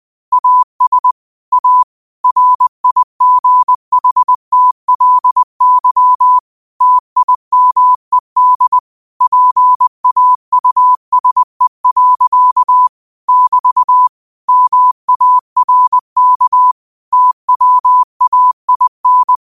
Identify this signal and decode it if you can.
Morse code, heard as a keyed tone, about 20 wpm, 1000 Hz. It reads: AS A RIGHTLY TIMED PAUSE. <BT> MARK TWAIN